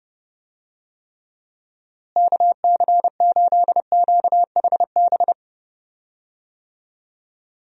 Morse code, heard as a keyed tone, about 30 wpm, 700 Hz. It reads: KC8QHB
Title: USA Call Signs - 6 Characters - Repeated 3x - 30wpm